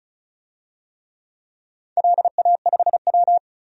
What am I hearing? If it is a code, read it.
Morse code, heard as a keyed tone, about 34 wpm, 700 Hz. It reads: LA5W
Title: DXCC-100 Call Signs - 4 Characters - 35wpm